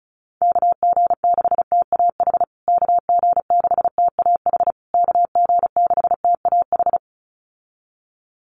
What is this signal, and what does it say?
Morse code, heard as a keyed tone, about 34 wpm, 700 Hz. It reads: KG6TAH KG6TAH KG6TAH
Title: USA Call Signs Repeated 3x 35wpm